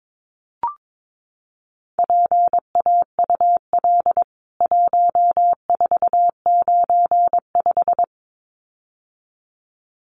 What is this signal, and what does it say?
Morse code, heard as a keyed tone, about 22 wpm, 700 Hz. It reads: PAUL 1495
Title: CWT Exchange 22wpm